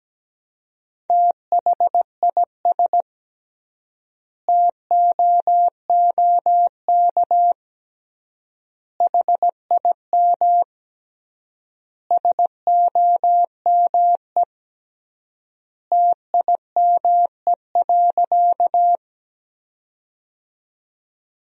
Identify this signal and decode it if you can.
Morse code, heard as a keyed tone, about 17 wpm, 700 Hz. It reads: THIS TOOK HIM SOME TIME.
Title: 3x Word-Spacing - Sentences from Top 300 Words - 17wpm